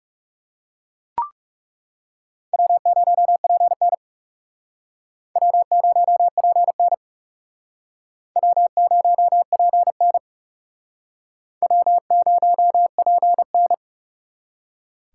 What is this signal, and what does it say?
Morse code, heard as a keyed tone, about 40 wpm, 700 Hz. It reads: W0PN W0PN W0PN W0PN